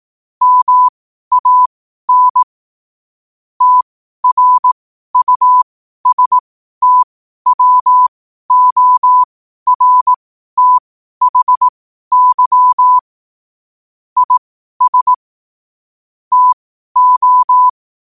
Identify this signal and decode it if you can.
Morse code, heard as a keyed tone, about 18 wpm, spaced as for 12 wpm, 1000 Hz. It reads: MAN TRUSTWORTHY IS TO
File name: qotdfiles / 12wpm